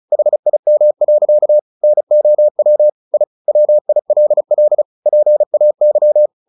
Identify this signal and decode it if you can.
Morse code, heard as a keyed tone, about 34 wpm, 600 Hz. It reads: HIM. NOW I WILL PAY
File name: the old man and the sea / the old man and the sea 1 35wpm